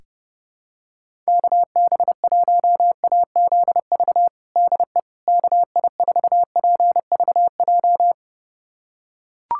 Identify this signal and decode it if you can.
Morse code, heard as a keyed tone, about 32 wpm, 700 Hz. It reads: KB1AZV DE KI4PVJ
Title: USA Callsign DE Callsign - 6 Characters - 30wpm